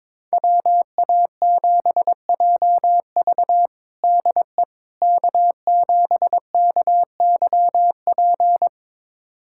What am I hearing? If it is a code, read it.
Morse code, heard as a keyed tone, about 21 wpm, 700 Hz. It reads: WA7JV DE K7KYP
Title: USA Callsign DE Callsign - 5 Characters - 22wpm